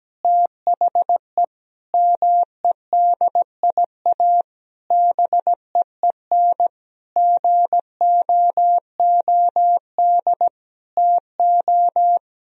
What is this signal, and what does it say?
Morse code, heard as a keyed tone, about 17 wpm, 700 Hz. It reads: THE MEDIA BEEN GOOD TO
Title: Sentences from Top 1000 Words - Review Part 4 - 17wpm